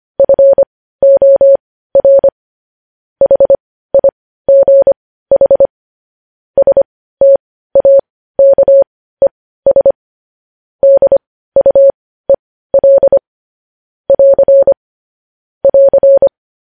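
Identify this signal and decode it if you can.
Morse code, heard as a keyed tone, about 25 wpm, spaced as for 15 wpm, 550 Hz. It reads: FOR HIGH STAKES DUEL <AR> <AR>
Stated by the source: Morse news headlines sent at 15 words per minute.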